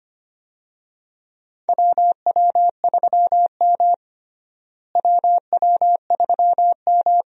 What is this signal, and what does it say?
Morse code, heard as a keyed tone, about 25 wpm, 700 Hz. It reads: WW3M WW3M
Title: USA Call Signs - 4 Characters - Repeated 2x - 25wpm